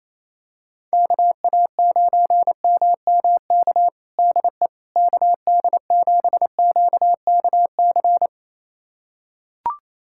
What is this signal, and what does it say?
Morse code, heard as a keyed tone, about 28 wpm, 700 Hz. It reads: KA9MMK DE KD7QKC
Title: USA Callsign DE Callsign 28wpm